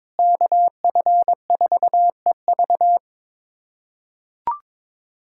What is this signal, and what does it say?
Morse code, heard as a keyed tone, about 22 wpm, 700 Hz. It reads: KF4EV
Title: USA Callsign DE Callsign - 5 Characters - 22wpm